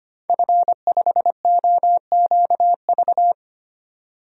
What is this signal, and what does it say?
Morse code, heard as a keyed tone, about 25 wpm, 700 Hz. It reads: F5OQV
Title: DXCC-100 Call Signs - 5 Characters - 25wpm